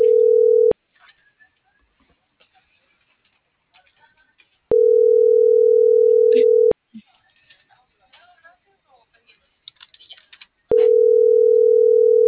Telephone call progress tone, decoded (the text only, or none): ringback tone